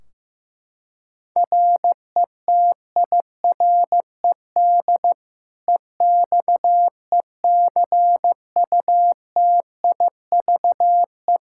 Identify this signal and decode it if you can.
Morse code, heard as a keyed tone, about 15 wpm, 700 Hz. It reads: RETIRED EXECUTIVE
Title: Practice QSO - Exchange 3 - More Info - 15wpm